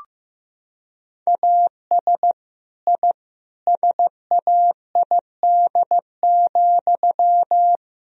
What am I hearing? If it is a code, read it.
Morse code, heard as a keyed tone, about 15 wpm, 700 Hz. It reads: AS I SAID,
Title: Sentences from Top 1000 Words - Review Part 3 - 15wpm